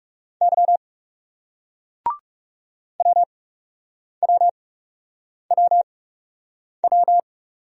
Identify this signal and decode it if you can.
Morse code, heard as a keyed tone, about 39 wpm, spaced as for 8 wpm, 700 Hz. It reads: Y WWWW